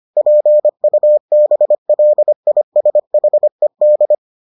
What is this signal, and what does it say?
Morse code, heard as a keyed tone, about 25 wpm, 600 Hz. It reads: PUBLISHED